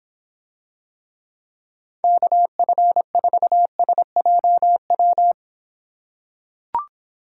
Morse code, heard as a keyed tone, about 26 wpm, 700 Hz. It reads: KF4SJW